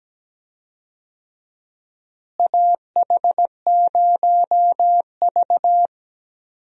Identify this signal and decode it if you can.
Morse code, heard as a keyed tone, about 17 wpm, 700 Hz. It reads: AH0V
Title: USA Call Signs - 4 Characters - Repeated 3x - 17wpm